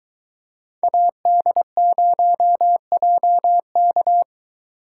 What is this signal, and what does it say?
Morse code, heard as a keyed tone, about 22 wpm, 700 Hz. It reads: AD0JK